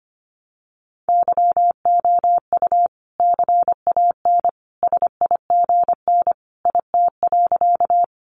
Morse code, heard as a keyed tone, about 25 wpm, 700 Hz. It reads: YOU CAN SIGN IT.
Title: Sentences from Top 1000 Words 25wpm